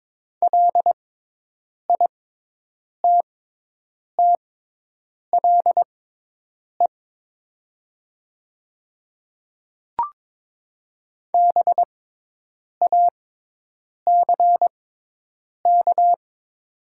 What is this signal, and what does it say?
Morse code, heard as a keyed tone, about 22 wpm, spaced as for 8 wpm, 700 Hz. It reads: LITTLE BACK
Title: 6x Character Spacing - Top 300 Words - 22wpm